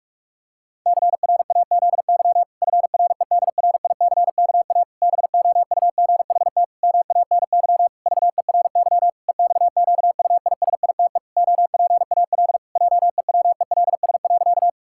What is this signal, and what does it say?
Morse code, heard as a keyed tone, about 46 wpm, 700 Hz. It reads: CRAZY FREDRICKA BOUGHT MANY VERY EXQUISITE OPAL JEWELS.
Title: Pangram Challenge v2 - 45wpm